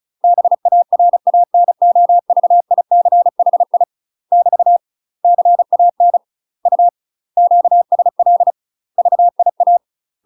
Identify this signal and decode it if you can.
Morse code, heard as a keyed tone, about 34 wpm, 700 Hz. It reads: BARANOVICHI <BT> CAN U QSL VIA
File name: UK MASTER QSOs TRAINING / 35WPM / UK MASTER QSOs 0008